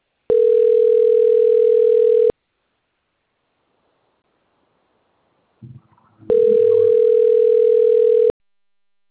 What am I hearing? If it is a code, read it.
Ringback tone (North America), 440 + 480 Hz.